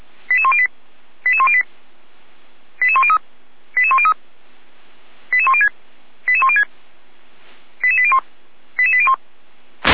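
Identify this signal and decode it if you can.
Five-tone selective call (ZVEI), heard as quick runs of five tones, tones of about 70 ms. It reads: 88108 88108 88103 88103 88106 88106 88801 88801
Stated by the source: ZVEI five-tone call received by SDR.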